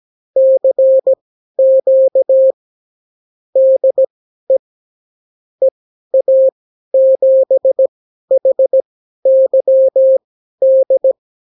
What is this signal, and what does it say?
Morse code, heard as a keyed tone, about 17 wpm, spaced as for 12 wpm, 530 Hz. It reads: CQ DE EA7HYD
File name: CW QSO libroCW4 18-8ppm